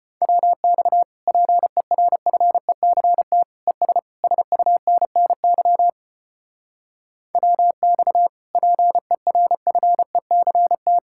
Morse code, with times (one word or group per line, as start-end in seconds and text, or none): 0.2-1.0 WX
1.3-3.4 PERFECT
3.7-4.0 ES
4.2-5.9 SUNNY
7.4-8.3 WX
8.6-11.0 PERFECT